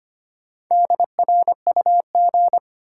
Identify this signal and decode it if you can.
Morse code, heard as a keyed tone, about 25 wpm, 700 Hz. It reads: DRUG
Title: Sets of 5 Words - New Word Drill - 25wpm